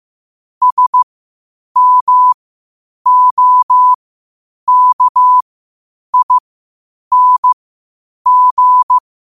Morse code, heard as a keyed tone, about 15 wpm, spaced as for 8 wpm, 1000 Hz. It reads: SMOKING
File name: qotdfiles / 8wpm